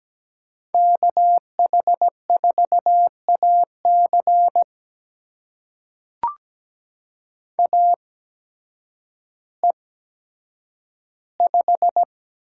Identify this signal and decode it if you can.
Morse code, heard as a keyed tone, about 17 wpm, spaced as for 9 wpm, 700 Hz. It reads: KH4AC AE5
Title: USA Call Signs - 5 Characters - 8x Character Spacing - 17wpm